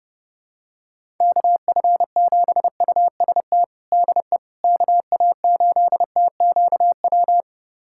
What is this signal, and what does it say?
Morse code, heard as a keyed tone, about 30 wpm, 700 Hz. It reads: KF7UST DE KA8TQW
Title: USA Callsign DE Callsign 30wpm